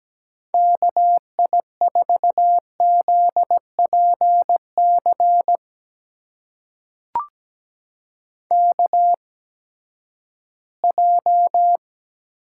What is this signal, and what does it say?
Morse code, heard as a keyed tone, about 17 wpm, spaced as for 12 wpm, 700 Hz. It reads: KI4ZPC KJ
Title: USA Call Signs - 6 Characters - 8x Character Spacing - 17wpm